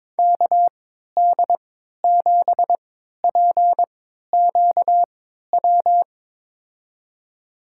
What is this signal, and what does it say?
Morse code, heard as a keyed tone, about 22 wpm, spaced as for 13 wpm, 700 Hz. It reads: KD7PQW
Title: USA Call Signs - 6 Characters - 3x Character Spacing - 22wpm